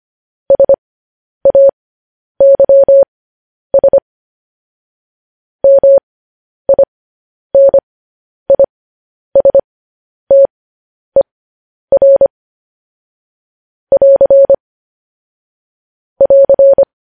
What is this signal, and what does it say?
Morse code, heard as a keyed tone, about 25 wpm, spaced as for 10 wpm, 550 Hz. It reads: SAYS MINISTER <AR> <AR>
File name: podcast / News headlines-1x-25x10